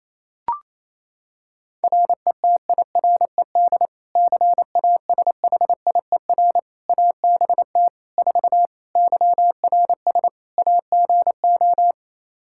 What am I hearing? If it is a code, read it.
Morse code, heard as a keyed tone, about 28 wpm, 700 Hz. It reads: RETIRED CASHIER ABT 4 YRS AGO
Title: Practice QSO - Exchange 3 - More Info - 28wpm